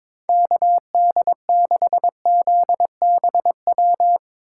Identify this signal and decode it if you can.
Morse code, heard as a keyed tone, about 22 wpm, 700 Hz. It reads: KD6ZBW
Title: USA Call Signs - 4x Character Spacing - 22wpm